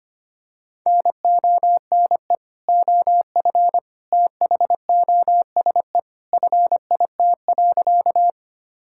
Morse code, heard as a keyed tone, about 25 wpm, 700 Hz. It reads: NONE OF THOSE FIT.
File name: practice / 25wpm / Sentences with 2-Word Phrases - 25wpm